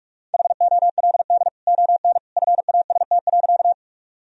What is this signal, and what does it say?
Morse code, heard as a keyed tone, about 46 wpm, 700 Hz. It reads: HOLD ON FAST.
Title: Sentences from Top 1000 Words - Review Part 2 - 45wpm